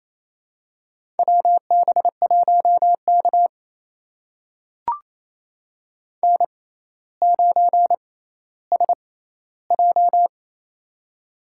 Morse code, heard as a keyed tone, about 28 wpm, spaced as for 14 wpm, 700 Hz. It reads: WB1K N9SJ
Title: USA Call Signs - 4 Characters - 6x Character Spacing - 28wpm